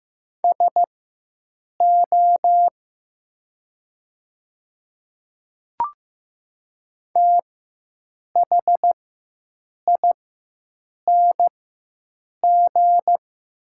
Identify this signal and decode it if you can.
Morse code, heard as a keyed tone, about 15 wpm, spaced as for 7 wpm, 700 Hz. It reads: SO THING
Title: 4x Character Spacing - Top 100 Words - 15wpm